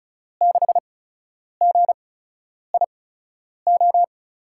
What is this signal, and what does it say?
Morse code, heard as a keyed tone, about 36 wpm, spaced as for 10 wpm, 700 Hz. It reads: 6GIO
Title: USA Call Signs - 5 Characters - 8x Character Spacing - 35wpm